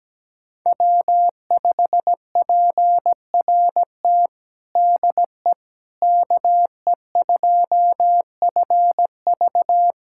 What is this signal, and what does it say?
Morse code, heard as a keyed tone, about 17 wpm, 700 Hz. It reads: W5PRT DE KE2FV
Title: USA Callsign DE Callsign - 5 Characters - 17wpm